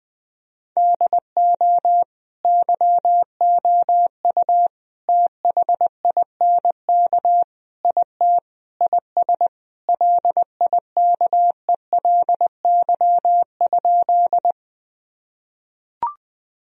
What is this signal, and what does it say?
Morse code, heard as a keyed tone, about 19 wpm, 700 Hz. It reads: DO YOU THINK IT IS LIKELY?
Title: Sentences from Top 1000 Words - Review Part 4 - 20wpm